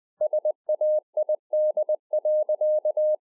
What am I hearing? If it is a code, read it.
Morse code, heard as a keyed tone, about 20 wpm, 610 Hz. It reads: SAID.